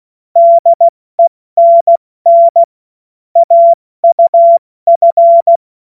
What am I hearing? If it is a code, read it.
Morse code, heard as a keyed tone, about 16 wpm, 680 Hz. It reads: DENN AUF
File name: Morsekurs / Witze / Witz 15. Juni 2021, 70 F80